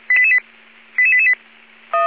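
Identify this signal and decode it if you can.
Five-tone selective call (ZVEI), heard as quick runs of five tones, tones of about 70 ms. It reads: 88888
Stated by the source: Five-tone ZVEI selective call received on an SDR.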